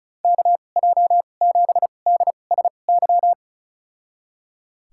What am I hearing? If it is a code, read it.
Morse code, heard as a keyed tone, about 36 wpm, spaced as for 25 wpm, 700 Hz. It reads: KJ7DSY
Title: USA Call Signs - 2x Character Spacing - 35wpm